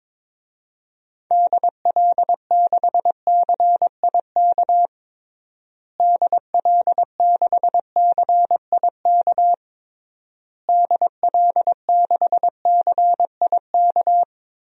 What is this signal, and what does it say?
Morse code, heard as a keyed tone, about 22 wpm, 700 Hz. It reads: DL6CIK DL6CIK DL6CIK
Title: DXCC-100 Call Signs - 6 Characters - Repeated 3x - 22wpm